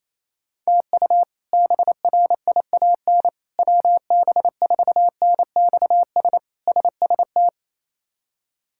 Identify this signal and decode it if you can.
Morse code, heard as a keyed tone, about 28 wpm, 700 Hz. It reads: TU BRIAN WB4NXS SST
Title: K1USN SST QSO Practice - 28wpm